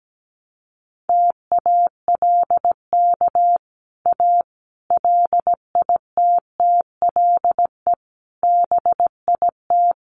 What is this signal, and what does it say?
Morse code, heard as a keyed tone, about 17 wpm, 700 Hz. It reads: TALK A LITTLE BIT
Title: Sets of 5 Words Encore 17wpm